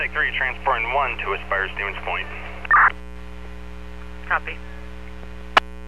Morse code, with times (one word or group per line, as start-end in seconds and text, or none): none